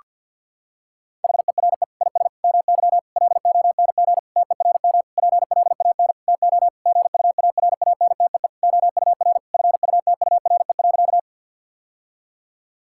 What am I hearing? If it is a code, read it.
Morse code, heard as a keyed tone, about 51 wpm, 700 Hz. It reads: HERE IS MY LONG TERM PLAN TO GUARANTEE OUR FUTURE.
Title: Sentences from Top 1600 Words 50wpm